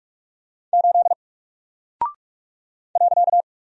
Morse code, heard as a keyed tone, about 46 wpm, 700 Hz. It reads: 8 .
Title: Single Character (Letter:Number:Punctuation:Prosign) 45wpm